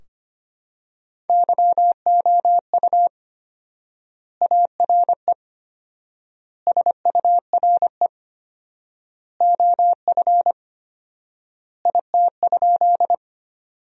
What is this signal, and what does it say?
Morse code, heard as a keyed tone, about 25 wpm, 700 Hz. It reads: YOU ARE SURE OF IT?
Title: 4x Word-Spacing - Sentences from Top 300 Words - 25wpm